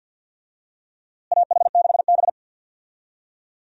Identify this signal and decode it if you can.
Morse code, heard as a keyed tone, about 47 wpm, 700 Hz. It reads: AH6B
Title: USA Call Signs - 4 Characters - Repeated 2x - 50wpm